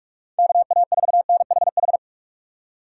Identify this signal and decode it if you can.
Morse code, heard as a keyed tone, about 44 wpm, 700 Hz. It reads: KA4NHH